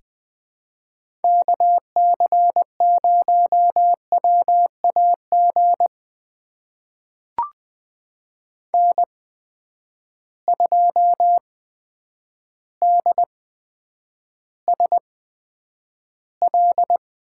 Morse code, heard as a keyed tone, about 20 wpm, spaced as for 9 wpm, 700 Hz. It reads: KC0WAG N2DSL
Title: USA Call Signs - 8x Character Spacing - 20wpm